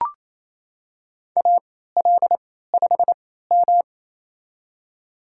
Morse code, heard as a keyed tone, about 27 wpm, spaced as for 16 wpm, 700 Hz. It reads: AL5M
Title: USA Call Signs - 4 Characters - 3x Character Spacing - 28wpm